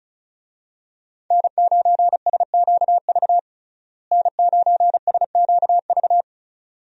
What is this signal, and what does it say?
Morse code, heard as a keyed tone, about 36 wpm, 700 Hz. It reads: N9SQV N9SQV
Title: USA Call Signs - 5 Characters - Repeated 2x - 35wpm